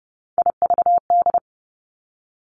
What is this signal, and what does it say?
Morse code, heard as a keyed tone, about 30 wpm, 700 Hz. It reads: IVD
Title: Three Challenging Letter-Number Combinations 30wpm